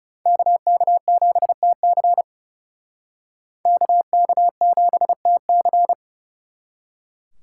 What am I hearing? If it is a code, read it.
Morse code, heard as a keyed tone, about 36 wpm, 700 Hz. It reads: KK7TC KK7TC
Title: Speed Racing - USA 2x2 Call Signs 30wpm